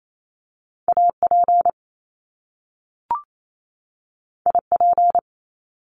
Morse code, heard as a keyed tone, about 28 wpm, 700 Hz. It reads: AP IP
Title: Top Two Letter Combinations in Words 28wpm